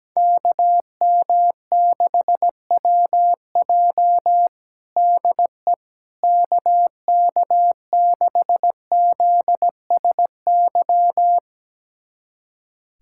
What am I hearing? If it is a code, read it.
Morse code, heard as a keyed tone, about 17 wpm, 700 Hz. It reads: KM6WJ DE KK6ZSY
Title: USA Callsign DE Callsign 17wpm